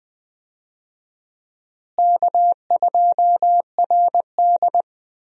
Morse code, heard as a keyed tone, about 20 wpm, 700 Hz. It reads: K2RD
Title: USA Call Signs - 4 Characters - Repeated 3x - 20wpm